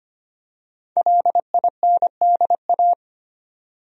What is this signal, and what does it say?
Morse code, heard as a keyed tone, about 25 wpm, 700 Hz. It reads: LINDA
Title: USA Common Names - 25wpm